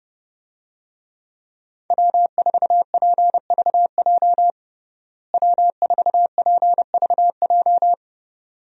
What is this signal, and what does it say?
Morse code, heard as a keyed tone, about 32 wpm, 700 Hz. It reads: W4PVJ W4PVJ
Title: USA Call Signs - 5 Characters - Repeated 2x - 30wpm